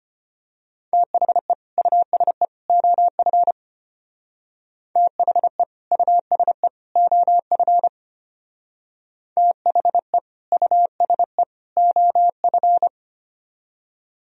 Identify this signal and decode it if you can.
Morse code, heard as a keyed tone, about 30 wpm, 700 Hz. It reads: THE USE OF THE USE OF THE USE OF
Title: Speed-Racing - Sets of 3 Words 25wpm